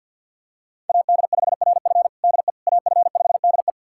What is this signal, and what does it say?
Morse code, heard as a keyed tone, about 52 wpm, 700 Hz. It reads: AD5RF DE AF5DE
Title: USA Callsign DE Callsign - 5 Characters - 50wpm